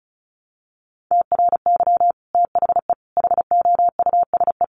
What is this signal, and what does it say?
Morse code, heard as a keyed tone, about 34 wpm, 700 Hz. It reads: TRY THE HOUSE
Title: Sentences from Top 1300 Words 35wpm